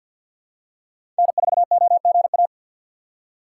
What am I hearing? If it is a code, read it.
Morse code, heard as a keyed tone, about 52 wpm, 700 Hz. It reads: N4OGA
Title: USA Call Signs - 5 Characters - 8x Character Spacing - 50wpm